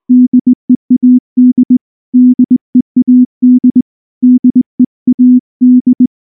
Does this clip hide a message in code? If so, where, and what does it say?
Yes: Morse code from 0.1 to 6.1 s, heard as a keyed tone, about 20 wpm, 250 Hz. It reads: DEAD DEAD DEAD